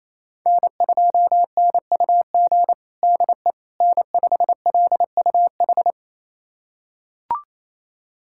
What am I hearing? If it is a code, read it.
Morse code, heard as a keyed tone, about 27 wpm, 700 Hz. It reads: N2NUG DE N5LUH
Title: USA Callsign DE Callsign - 5 Characters - 28wpm